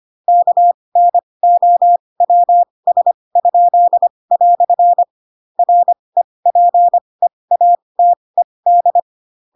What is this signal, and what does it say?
Morse code, heard as a keyed tone, about 25 wpm, spaced as for 20 wpm, 700 Hz. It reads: KNOWS?" REPEATED
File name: Pinocchio / Pinocchio ch7 15 25 ik1hge